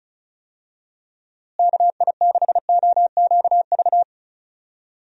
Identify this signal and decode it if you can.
Morse code, heard as a keyed tone, about 34 wpm, 700 Hz. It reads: KI6OQV
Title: USA Call Signs - 6 Characters - Repeated 2x - 35wpm